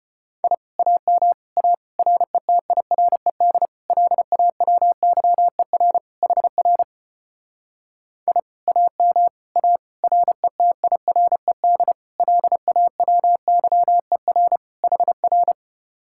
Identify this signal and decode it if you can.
Morse code, heard as a keyed tone, about 33 wpm, 700 Hz. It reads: I AM A RETIRED LAWYER HR I AM A RETIRED LAWYER HR